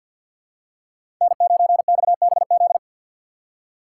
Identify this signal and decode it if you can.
Morse code, heard as a keyed tone, about 47 wpm, 700 Hz. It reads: N9XBZ
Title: USA Call Signs - 3x Character Spacing - 50wpm